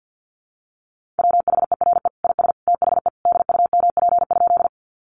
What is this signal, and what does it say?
Morse code, heard as a keyed tone, about 50 wpm, 700 Hz. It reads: WHERE IS THE DUMP?
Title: Sentences from Top 1300 Words 50wpm